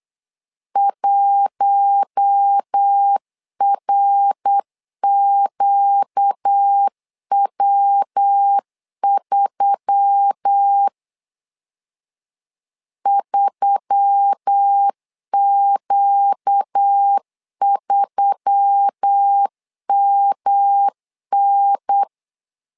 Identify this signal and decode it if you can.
Morse code, heard as a keyed tone, about 8 wpm, 800 Hz. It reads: 1RQW3 3Q3MN
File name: LEZIONI / 4 Q1S3 8wpm 3037